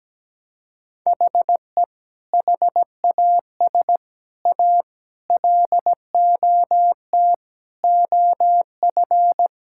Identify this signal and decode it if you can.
Morse code, heard as a keyed tone, about 17 wpm, 700 Hz. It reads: HE HAS A LOT OF
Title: Sentences from Top 1000 Words - Review Part 6 - 17wpm